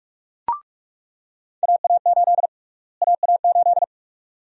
Morse code, heard as a keyed tone, about 46 wpm, 700 Hz. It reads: AA8 AA8
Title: USA Call Sign Prefixes - Repeated 2x - 45wpm